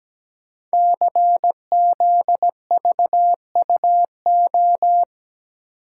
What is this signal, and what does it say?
Morse code, heard as a keyed tone, about 17 wpm, 700 Hz. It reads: CZVUO
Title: Code Groups - 5 Letter:Numbers - 17wpm